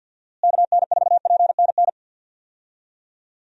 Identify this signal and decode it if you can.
Morse code, heard as a keyed tone, about 51 wpm, 700 Hz. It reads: KN4PNN
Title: USA Call Signs - 6 Characters - Repeated 3x - 50wpm